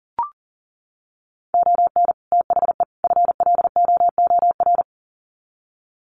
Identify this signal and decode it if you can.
Morse code, heard as a keyed tone, about 39 wpm, 700 Hz. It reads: ON THE FLOOR
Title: Sets of 3 Words 40wpm